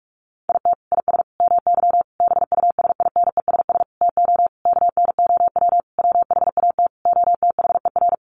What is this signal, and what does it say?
Morse code, heard as a keyed tone, about 44 wpm, 700 Hz. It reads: IT IS MY BUSINESS TO KNOW WHAT OTHER
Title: Sentences from Top 600 Words 45wpm